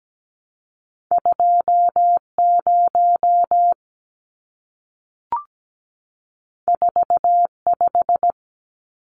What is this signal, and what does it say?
Morse code, heard as a keyed tone, about 17 wpm, 700 Hz. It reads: 20 45
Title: Numbers - 2-Digit 17wpm